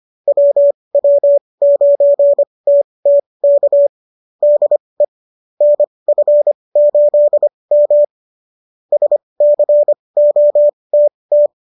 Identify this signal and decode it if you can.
Morse code, heard as a keyed tone, about 25 wpm, spaced as for 20 wpm, 590 Hz. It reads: WW9TT K DE NF8M SCOTT
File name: qso223 20